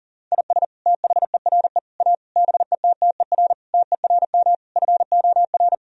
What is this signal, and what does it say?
Morse code, heard as a keyed tone, about 40 wpm, 700 Hz. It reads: IS THERE A BETTER TERM FOR
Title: Sentences from Top 1000 Words - Review Part 4 - 40wpm